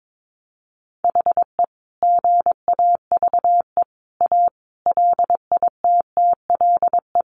Morse code, heard as a keyed tone, about 22 wpm, 700 Hz. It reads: HE GAVE A LITTLE
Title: Sentences from Top 1400 Words 22wpm